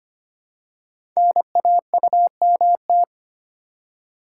Morse code, heard as a keyed tone, about 25 wpm, 700 Hz. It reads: NAUMT
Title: Code Groups - 5 Letters - 25wpm